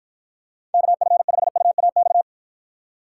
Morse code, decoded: KF5UAX

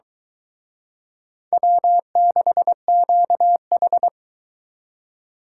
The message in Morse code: W6QH

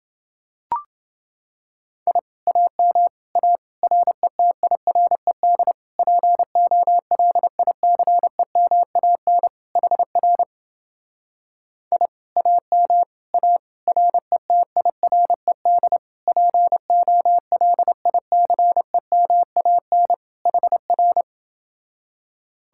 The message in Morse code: I AM A RETIRED POLICEMAN HR I AM A RETIRED POLICEMAN HR

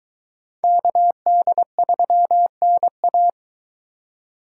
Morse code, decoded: KD3NA

